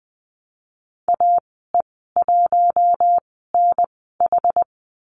Morse code, heard as a keyed tone, about 20 wpm, spaced as for 14 wpm, 700 Hz. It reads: AE1NH
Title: 062 Lesson 11g - Letter h - Call Signs 20wpm